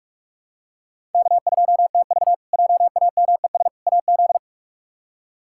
Morse code, heard as a keyed tone, about 41 wpm, 700 Hz. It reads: K2TV JAMES AZ
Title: K1USN SST QSO Practice - 45wpm